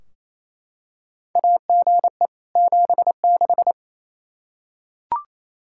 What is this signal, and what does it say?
Morse code, heard as a keyed tone, about 28 wpm, 700 Hz. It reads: AGE 76
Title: Practice QSO - Exchange 3 - More Info - 28wpm